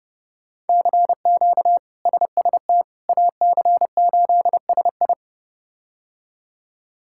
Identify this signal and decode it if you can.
Morse code, heard as a keyed tone, about 30 wpm, 700 Hz. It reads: CQ SST AC8SI
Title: K1USN SST QSO Practice - 30wpm